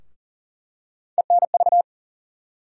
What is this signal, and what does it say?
Morse code, heard as a keyed tone, about 40 wpm, 700 Hz. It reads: ENV
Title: USA Call Sign Postfixes - 3 Letters 40wpm